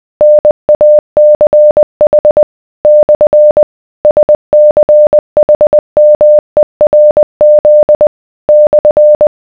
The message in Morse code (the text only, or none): NACH / SCHMERZ /